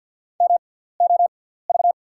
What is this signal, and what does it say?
Morse code, heard as a keyed tone, about 54 wpm, spaced as for 17 wpm, 690 Hz. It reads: MOV
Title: 6x Character Spacing - Top 300 Words - 50wpm